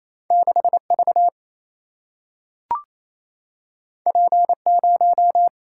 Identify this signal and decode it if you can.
Morse code, heard as a keyed tone, about 28 wpm, 700 Hz. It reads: 6 V P 0
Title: Single-Letter Single-Number 28wpm